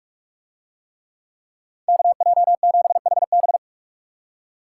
Morse code, heard as a keyed tone, about 45 wpm, 700 Hz. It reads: KJ7HB